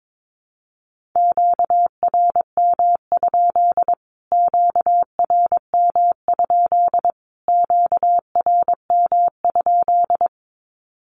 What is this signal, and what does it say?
Morse code, heard as a keyed tone, about 22 wpm, 700 Hz. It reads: QRM? QRM? QRM?
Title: Q-Codes Repeated 3x 22wpm